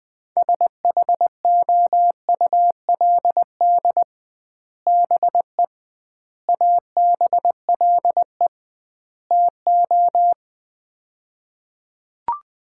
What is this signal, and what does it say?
Morse code, heard as a keyed tone, about 20 wpm, 700 Hz. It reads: SHOULD BE ABLE TO
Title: 2x Word-Spacing - Sets of 4 Words - 20wpm